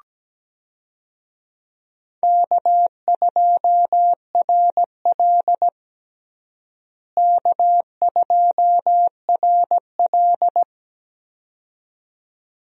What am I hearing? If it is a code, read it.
Morse code, heard as a keyed tone, about 17 wpm, 700 Hz. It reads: K2RL K2RL
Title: USA Call Signs - 4 Characters - Repeated 2x - 17wpm